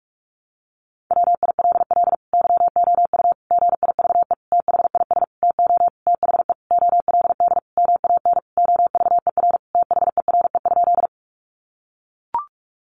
Morse code, heard as a keyed tone, about 44 wpm, 700 Hz. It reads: WILL YOU GIVE THIS TO THE OLD MAN OVER THERE?
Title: Sentences from Top 200 Words 45wpm